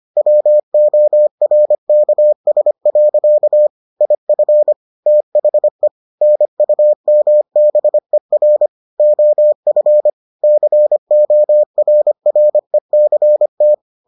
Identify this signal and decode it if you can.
Morse code, heard as a keyed tone, about 25 wpm, 600 Hz. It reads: WORKS. IF THE NUMBER OF CORRECT